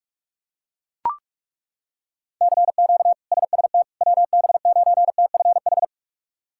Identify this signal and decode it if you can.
Morse code, heard as a keyed tone, about 42 wpm, 700 Hz. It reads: CQ SST WB9TFH
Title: K1USN SST QSO Practice - 45wpm